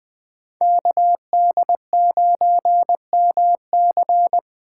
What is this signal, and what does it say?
Morse code, heard as a keyed tone, about 20 wpm, 700 Hz. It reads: KD9MC
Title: USA Call Signs - 5 Characters - 2x Character Spacing - 20wpm